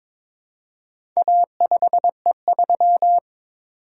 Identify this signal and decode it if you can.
Morse code, heard as a keyed tone, about 22 wpm, 690 Hz. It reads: A5E3
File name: practice / 22wpm / Four Letter-Number Combinations - 22wpm